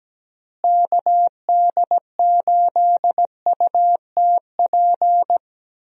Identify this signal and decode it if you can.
Morse code, heard as a keyed tone, about 17 wpm, 700 Hz. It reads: KD8UTP
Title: USA Call Signs - 6 Characters - Repeated 2x - 17wpm